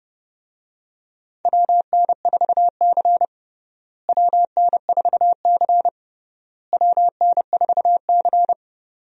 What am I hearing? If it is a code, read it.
Morse code, heard as a keyed tone, about 30 wpm, 700 Hz. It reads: WN4C WN4C WN4C
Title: USA Call Signs - 4 Characters - Repeated 3x - 30wpm